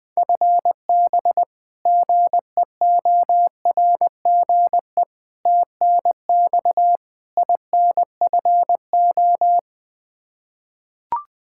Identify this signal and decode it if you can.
Morse code, heard as a keyed tone, about 20 wpm, 700 Hz. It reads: FB GEORGE TNX INFO
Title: K1USN SST QSO Practice 20wpm